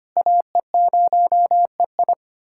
Morse code, heard as a keyed tone, about 25 wpm, 700 Hz. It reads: AE0EI
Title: USA Callsign DE Callsign - 5 Characters - 25wpm